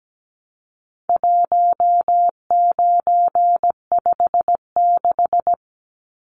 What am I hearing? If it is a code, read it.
Morse code, heard as a keyed tone, about 17 wpm, 700 Hz. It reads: 1956
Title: Numbers - 4-Digit 17wpm